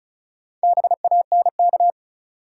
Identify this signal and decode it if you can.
Morse code, heard as a keyed tone, about 36 wpm, 700 Hz. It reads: BANK
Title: Sets of 5 Words - New Word Drill - 35wpm